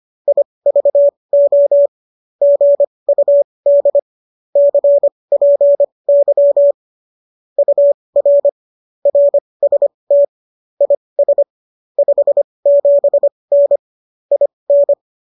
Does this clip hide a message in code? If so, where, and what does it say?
Yes: Morse code from 0.3 to 14.9 s, heard as a keyed tone, about 25 wpm, spaced as for 20 wpm, 570 Hz. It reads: IVO GUD CPY UR RST IS 57N IN